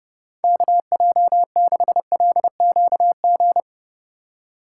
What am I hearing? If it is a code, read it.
Morse code, heard as a keyed tone, about 31 wpm, 700 Hz. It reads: KJ6LQG
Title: USA Call Signs - 6 Characters - 4x Character Spacing - 30wpm